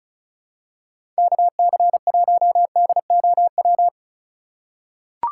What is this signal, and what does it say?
Morse code, heard as a keyed tone, about 34 wpm, 700 Hz. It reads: KC1DOW
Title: USA Call Signs - 6 Characters - 4x Character Spacing - 35wpm